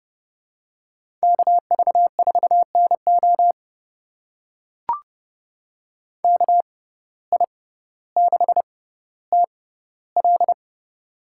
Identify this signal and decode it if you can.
Morse code, heard as a keyed tone, about 30 wpm, spaced as for 15 wpm, 700 Hz. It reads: KV4NO KI6TL
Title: USA Call Signs - 5 Characters - 6x Character Spacing - 30wpm